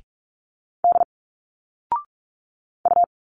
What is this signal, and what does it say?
Morse code, heard as a keyed tone, about 45 wpm, 700 Hz. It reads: D U
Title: Single Letters 45wpm